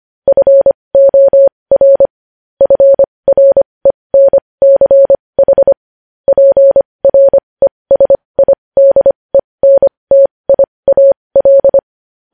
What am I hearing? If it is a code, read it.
Morse code, heard as a keyed tone, about 25 wpm, spaced as for 20 wpm, 550 Hz. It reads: FOR FRENCH PRESIDENTIAL
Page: news headlines in morse code at 20 wpm